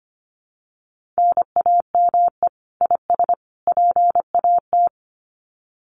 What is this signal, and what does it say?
Morse code, heard as a keyed tone, about 25 wpm, 700 Hz. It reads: NAME IS PAT
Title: Common QSO Phrases 25wpm